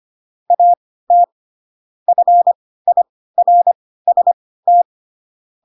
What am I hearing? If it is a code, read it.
Morse code, heard as a keyed tone, about 25 wpm, spaced as for 16 wpm, 700 Hz. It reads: AT FIRST
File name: Pinocchio / Pinocchio ch7 10 25 ik1hge